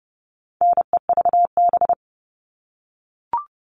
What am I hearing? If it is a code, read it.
Morse code, heard as a keyed tone, about 30 wpm, 700 Hz. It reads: NEVB
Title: All Cut Numbers - 4-Digit 30wpm